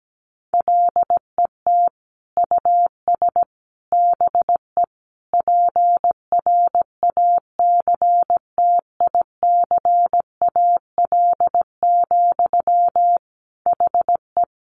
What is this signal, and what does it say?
Morse code, heard as a keyed tone, about 17 wpm, 700 Hz. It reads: LET US BE PRACTICAL, HE
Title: Sentences from Top 1400 Words 17wpm